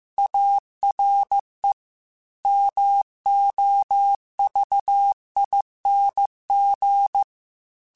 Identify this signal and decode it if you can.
Morse code, heard as a keyed tone, about 15 wpm, 800 Hz. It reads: ARE MOVING
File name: QSOs / QSOs-mp3s / WD0BGP 1@10